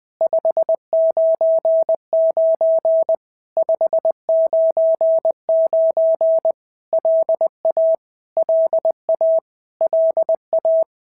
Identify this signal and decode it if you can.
Morse code, heard as a keyed tone, about 20 wpm, 650 Hz. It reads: 599 599 LA LA LA
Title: POTA QSO Practice - 20wpm